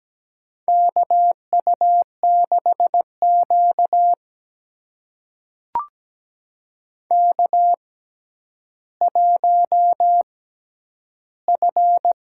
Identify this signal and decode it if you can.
Morse code, heard as a keyed tone, about 17 wpm, spaced as for 10 wpm, 700 Hz. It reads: KU6Q K1F
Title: USA Call Signs - 4 Characters - 6x Character Spacing - 17wpm